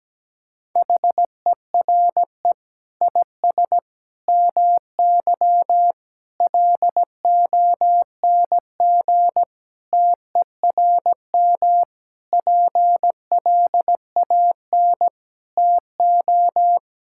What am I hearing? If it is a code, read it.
Morse code, heard as a keyed tone, about 17 wpm, 700 Hz. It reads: HERE IS MY LONG TERM PLAN TO
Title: Sentences from Top 1600 Words 17wpm